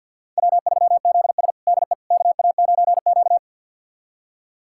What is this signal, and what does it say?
Morse code, heard as a keyed tone, about 52 wpm, 700 Hz. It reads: W3ZS DE KA9Q
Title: USA Callsign DE Callsign - 4 Characters - 50wpm